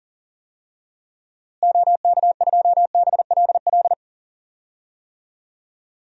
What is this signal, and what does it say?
Morse code, heard as a keyed tone, about 40 wpm, 700 Hz. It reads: OK2BLL
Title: DXCC-100 Call Signs - 6 Characters - Repeated 3x - 40wpm